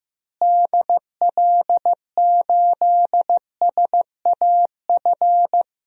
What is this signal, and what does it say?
Morse code, heard as a keyed tone, about 15 wpm, 700 Hz. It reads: DL8SAF
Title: DXCC-100 Call Signs - 6 Characters - Repeated 3x - 15wpm